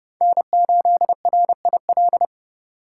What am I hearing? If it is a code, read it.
Morse code, heard as a keyed tone, about 28 wpm, 700 Hz. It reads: N8RIL